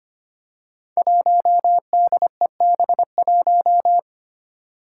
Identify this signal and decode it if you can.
Morse code, heard as a keyed tone, about 25 wpm, 700 Hz. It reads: 1DEB1
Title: Five Challenging Letter-Number Combinations 25wpm